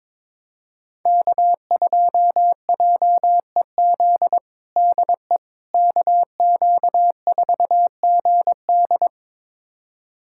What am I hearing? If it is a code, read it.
Morse code, heard as a keyed tone, about 22 wpm, 700 Hz. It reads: K2JEZ DE KQ4GD
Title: USA Callsign DE Callsign - 5 Characters - 22wpm